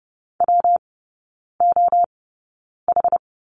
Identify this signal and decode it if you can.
Morse code, heard as a keyed tone, about 30 wpm, spaced as for 9 wpm, 700 Hz. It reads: WOH
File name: Lesson 19e - Letter w - Three Characters 30wpm